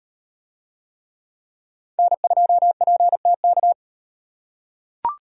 Morse code, heard as a keyed tone, about 38 wpm, 700 Hz. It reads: N2PTK